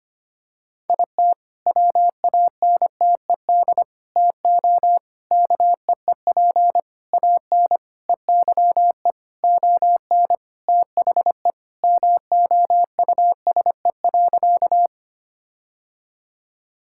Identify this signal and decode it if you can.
Morse code, heard as a keyed tone, about 25 wpm, 700 Hz. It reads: IT WANTED TO KEEP AN EYE ON THE MOUSE.
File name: practice / 25wpm / All-Age Appropriate Jokes - 25wpm